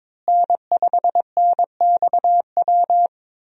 Morse code, heard as a keyed tone, about 21 wpm, 700 Hz. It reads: N5NXW
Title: USA Call Signs - 5 Characters - Repeated 2x - 22wpm